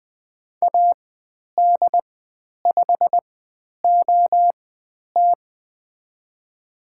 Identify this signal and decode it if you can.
Morse code, heard as a keyed tone, about 20 wpm, spaced as for 10 wpm, 700 Hz. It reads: AD5OT
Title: 102 Lesson 17g - Letter u - Call Signs 20wpm